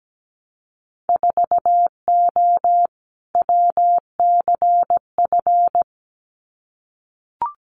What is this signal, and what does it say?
Morse code, heard as a keyed tone, about 17 wpm, 700 Hz. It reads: 4O WCF
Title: Winter Field Day Exchanges Encore 17wpm